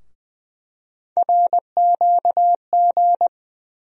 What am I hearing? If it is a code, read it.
Morse code, heard as a keyed tone, about 20 wpm, 700 Hz. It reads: RQG
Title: USA Call Sign Postfixes - 3 Letters 20wpm